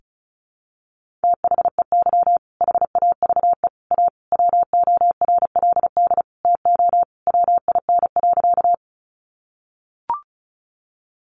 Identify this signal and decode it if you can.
Morse code, heard as a keyed tone, about 34 wpm, 700 Hz. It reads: THEY HAVE A WORLD TO WIN.
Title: Sentences from Top 1500 Words 35wpm